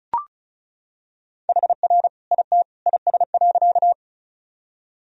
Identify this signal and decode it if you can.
Morse code, heard as a keyed tone, about 35 wpm, 700 Hz. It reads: HR IT IS.
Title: Sentences from Top 200 Words and Abbreviations 35wpm